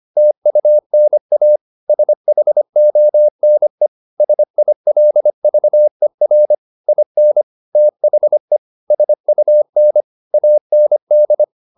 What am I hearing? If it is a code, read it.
Morse code, heard as a keyed tone, about 25 wpm, 600 Hz. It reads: TUNA SHONE SILVER IN THE SUN AND
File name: the old man and the sea part 1 25wpm